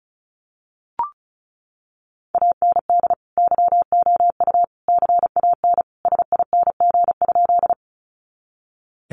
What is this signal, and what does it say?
Morse code, heard as a keyed tone, about 34 wpm, 700 Hz. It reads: AND YOU CAN SING?
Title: Sentences from Top 400 Words 35wpm